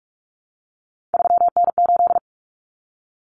Morse code, heard as a keyed tone, about 44 wpm, 700 Hz. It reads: 3N8
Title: Common Cut Numbers - 3-Digit 45wpm